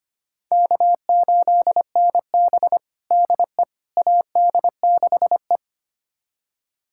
Morse code, heard as a keyed tone, about 25 wpm, 700 Hz. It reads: K8NB DE AD6E
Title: USA Callsign DE Callsign - 4 Characters - 25wpm